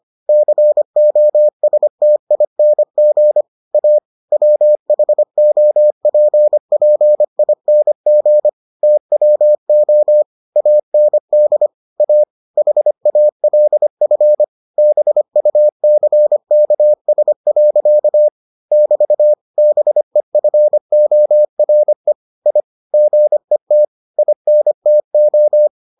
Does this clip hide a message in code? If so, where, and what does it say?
Yes: Morse code from 0.3 to 25.7 s, heard as a keyed tone, about 25 wpm, 600 Hz. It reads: COSTING A WHOPPING TWO AND A HALF BUCKS. <BT> BEFORE I GET INTO